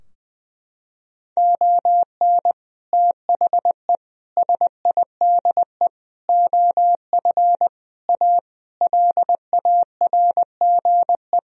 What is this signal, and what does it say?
Morse code, heard as a keyed tone, about 19 wpm, 700 Hz. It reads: ON THE SIDE OF A LARGE
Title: Sentences from Top 1000 Words - Review Part 3 - 20wpm